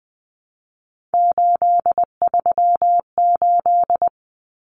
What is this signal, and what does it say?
Morse code, heard as a keyed tone, about 19 wpm, 700 Hz. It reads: 838
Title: Numbers - 3-Digit 20wpm